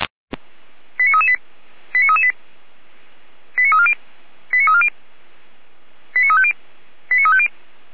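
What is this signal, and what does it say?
Five-tone selective call (ZVEI), heard as quick runs of five tones, tones of about 70 ms. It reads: 89338 89338 89340 89340 89350 89350
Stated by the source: ZVEI five-tone selective call received via SDR.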